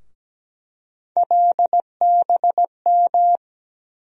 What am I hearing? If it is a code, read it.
Morse code, heard as a keyed tone, about 17 wpm, 700 Hz. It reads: LBM
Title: USA Call Sign Postfixes - 3 Letters 17wpm